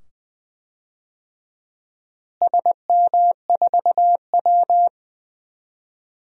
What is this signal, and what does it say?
Morse code, heard as a keyed tone, about 20 wpm, 700 Hz. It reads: SM4W
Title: DXCC-100 Call Signs - 4 Characters - Repeated 2x - 20wpm